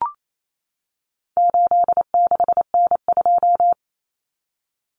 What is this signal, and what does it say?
Morse code, heard as a keyed tone, about 28 wpm, 700 Hz. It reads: 86N2
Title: Common Cut Numbers - 4-Digit 28wpm